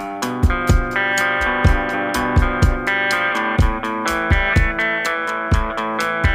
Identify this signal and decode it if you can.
Emergency Alert System (SAME) tones (header bursts), heard as harsh buzzing bursts. It is heard from 0.9 to 5.4 s.